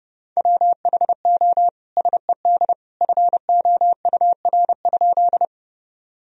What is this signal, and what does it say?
Morse code, heard as a keyed tone, about 29 wpm, 700 Hz. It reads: WHO SED FOUR?
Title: Sentences from Top 300 Words and Abbreviations 30wpm